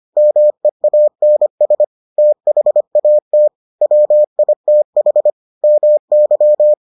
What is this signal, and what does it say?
Morse code, heard as a keyed tone, about 25 wpm, 600 Hz. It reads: MEANS THAT WITH MY